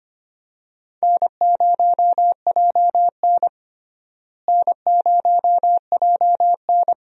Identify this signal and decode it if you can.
Morse code, heard as a keyed tone, about 25 wpm, 700 Hz. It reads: N0JN N0JN
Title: USA Call Signs - 4 Characters - Repeated 2x - 25wpm